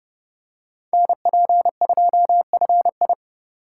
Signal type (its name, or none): Morse code